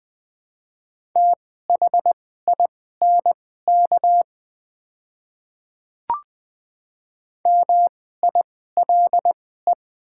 Morse code, heard as a keyed tone, about 20 wpm, spaced as for 14 wpm, 700 Hz. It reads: THINK MILE
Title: 2x Character Spacing - Top 300 Words - 20wpm